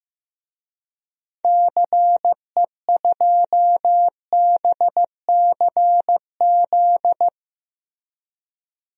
Morse code, heard as a keyed tone, about 15 wpm, 700 Hz. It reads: CE2BCZ
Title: DXCC-100 Call Signs - 6 Characters - 15wpm